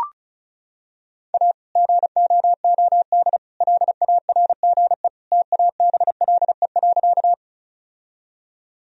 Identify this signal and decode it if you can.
Morse code, heard as a keyed tone, about 34 wpm, 700 Hz. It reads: A GOOD LARGE TABLE.
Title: Sentences from Top 1000 Words - Review Part 2 - 35wpm